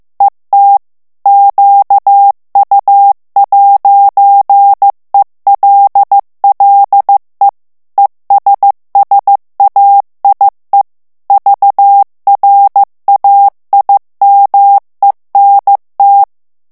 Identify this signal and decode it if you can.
Morse code, heard as a keyed tone, about 15 wpm, 800 Hz. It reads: ET QU'ELLE ESSAIE VRAIMENT
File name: morse mp3 / FRANCAIS TEXTE CLAIR 1 VITESSE 14 MOTS MINUTES